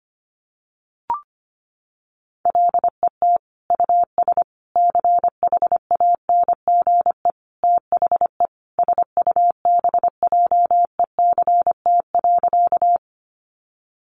Morse code, heard as a keyed tone, about 25 wpm, 700 Hz. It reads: LET US CHANGE THE SUBJECT.
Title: Sentences from Top 800 Words 25wpm